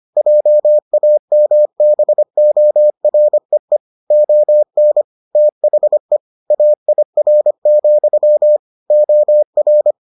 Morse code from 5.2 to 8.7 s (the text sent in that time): THE AIR,